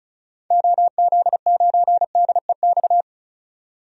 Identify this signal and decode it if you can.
Morse code, heard as a keyed tone, about 34 wpm, 700 Hz. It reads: OZ9DEX
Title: DXCC-100 Call Signs - 6 Characters - 35wpm